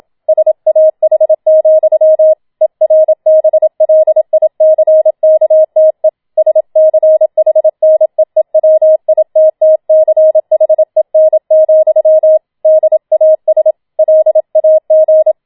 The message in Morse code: SAH, ERBLICKTE SCHNEEWITTCHEN, DAS LAG

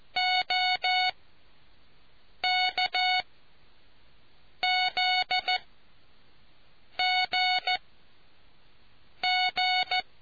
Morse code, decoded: OKZGG

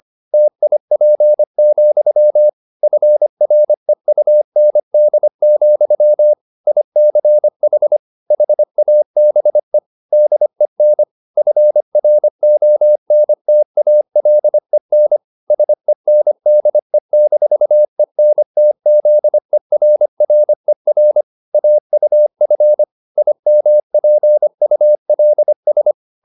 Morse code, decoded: TIP, FREUND, ICH HABE DEN FRONTALEN SENDE-ENTZERRER AUF IMPULS

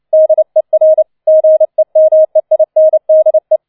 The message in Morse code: DER GEMEINDE

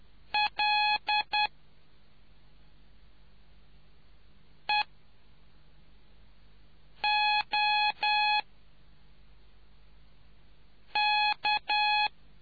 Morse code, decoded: L E O K

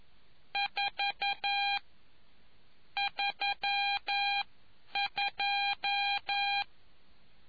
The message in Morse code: 4 32

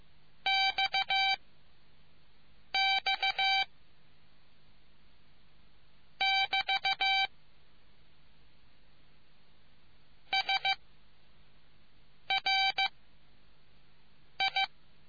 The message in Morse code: X X <BT> S R I

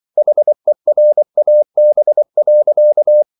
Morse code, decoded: HERAB.

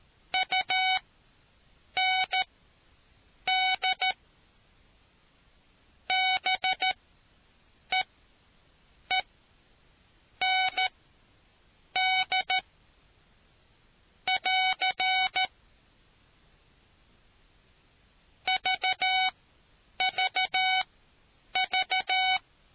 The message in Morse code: UND BEEND <AR> VVV

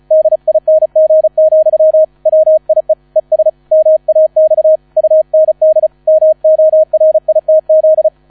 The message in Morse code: DING, WIE ES MAX UND MORITZ

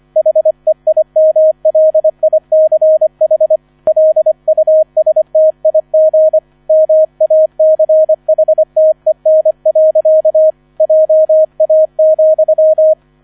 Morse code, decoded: HEIMLICH LUSTIG MACHTEN. JA,

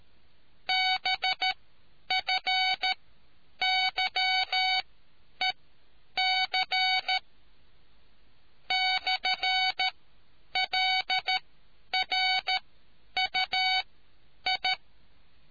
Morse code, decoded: BFYEC /LRUI

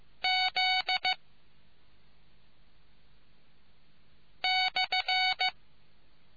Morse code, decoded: Z /

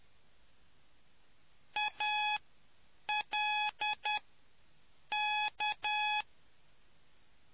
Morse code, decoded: ALK

